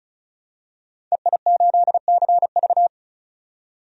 Morse code, heard as a keyed tone, about 34 wpm, 700 Hz. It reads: EI8CV
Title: DXCC-100 Call Signs - 5 Characters - 35wpm